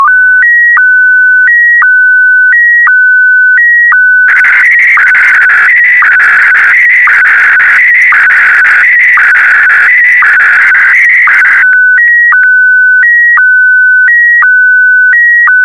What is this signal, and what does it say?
SSTV (slow-scan television), Scottie DX.